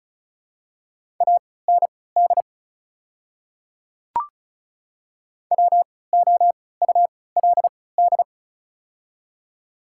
Morse code, heard as a keyed tone, about 36 wpm, spaced as for 20 wpm, 700 Hz. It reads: AND WOULD
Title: 3x Character Spacing - Top 300 Words - 35wpm